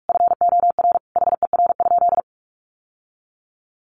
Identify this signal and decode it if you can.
Morse code, heard as a keyed tone, about 44 wpm, 700 Hz. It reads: FOR HER?
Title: Sentences from Top 800 Words 45wpm